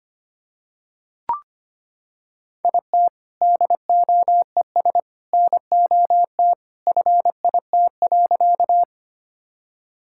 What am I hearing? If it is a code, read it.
Morse code, heard as a keyed tone, about 25 wpm, 700 Hz. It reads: IT DOES NOT FIT.
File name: practice / 25wpm / Sentences with 2-Word Phrases - 25wpm